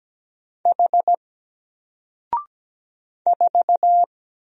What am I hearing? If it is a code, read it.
Morse code, heard as a keyed tone, about 17 wpm, 700 Hz. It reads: H 4
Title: Single Letter-Number Alt 17wpm